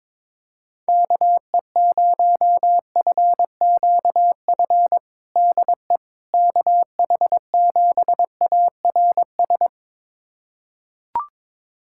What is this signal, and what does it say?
Morse code, heard as a keyed tone, about 22 wpm, 700 Hz. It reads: KE0FQF DE KH7ARS
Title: USA Callsign DE Callsign - 6 Characters - 22wpm